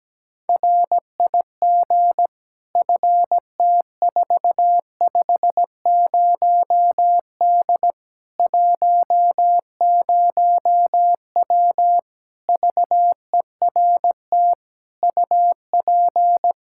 Morse code, heard as a keyed tone, about 17 wpm, 700 Hz. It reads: RIG FT450D 10W VERT UP
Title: Practice QSO - Exchange 2 - Info - 17wpm